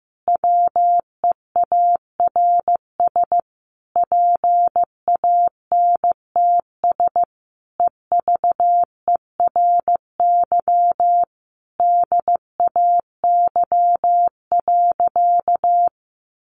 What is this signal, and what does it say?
Morse code, heard as a keyed tone, about 15 wpm, 700 Hz. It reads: WEARS PANTS EVERY DAY.
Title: Additional Easy Phrases 15wpm